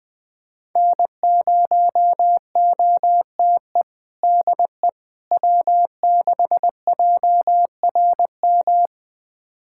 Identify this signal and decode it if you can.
Morse code, heard as a keyed tone, about 21 wpm, 700 Hz. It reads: N0OTE DE W6JRM
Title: USA Callsign DE Callsign - 5 Characters - 20wpm